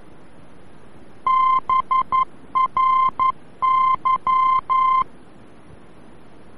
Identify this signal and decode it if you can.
Morse code, heard as a keyed tone, about 11 wpm, 1040 Hz. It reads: BRY
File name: BRY-VOR-F